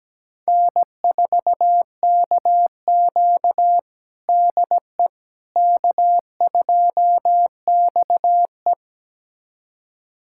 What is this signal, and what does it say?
Morse code, heard as a keyed tone, about 17 wpm, 700 Hz. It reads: N4KQ DE K2XE
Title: USA Callsign DE Callsign - 4 Characters - 17wpm